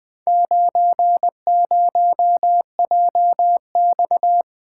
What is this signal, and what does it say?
Morse code, heard as a keyed tone, about 20 wpm, 700 Hz. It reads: 90JX
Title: Four Letter-Number Combinations - 20wpm